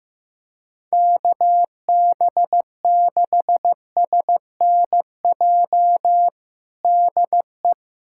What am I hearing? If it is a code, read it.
Morse code, heard as a keyed tone, about 15 wpm, 700 Hz. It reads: KB6SNJ DE
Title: USA Callsign DE Callsign - 6 Characters - 15wpm